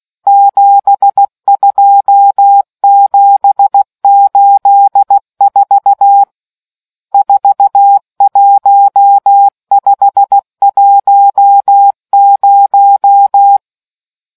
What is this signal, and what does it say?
Morse code, heard as a keyed tone, about 16 wpm, 800 Hz. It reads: 72784 41510